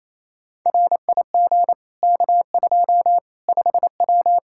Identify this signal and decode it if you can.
Morse code, heard as a keyed tone, about 28 wpm, 700 Hz. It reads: RIG K2 5W
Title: Practice QSO - Exchange 2 - Info - 28wpm